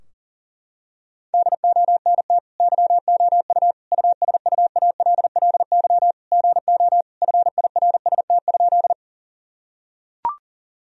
Morse code, heard as a keyed tone, about 40 wpm, 700 Hz. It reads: DONT YOU USUALLY GO FIRST?
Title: Sentences from Top 1000 Words - Review Part 3 - 40wpm